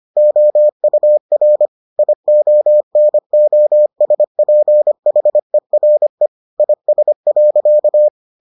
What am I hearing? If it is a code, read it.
Morse code, heard as a keyed tone, about 25 wpm, 600 Hz. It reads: OUR IONOSPHERE IS.